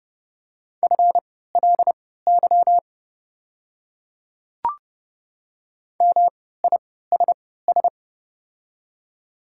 Morse code, heard as a keyed tone, about 30 wpm, spaced as for 17 wpm, 690 Hz. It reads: FLY MISS
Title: 3x Character Spacing - Top 500 Words - 30wpm